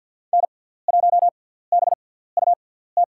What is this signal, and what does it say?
Morse code, heard as a keyed tone, about 47 wpm, spaced as for 17 wpm, 700 Hz. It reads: N1BUT
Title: USA Call Signs - 5 Characters - 6x Character Spacing - 50wpm